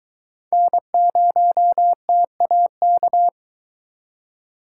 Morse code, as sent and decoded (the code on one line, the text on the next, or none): -. ----- - .- -.-
N0TAK